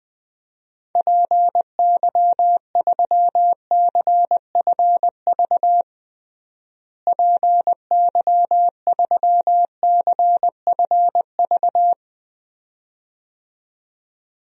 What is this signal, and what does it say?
Morse code, heard as a keyed tone, about 20 wpm, 700 Hz. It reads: PY3CFV PY3CFV
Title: DXCC-100 Call Signs - 6 Characters - Repeated 2x - 20wpm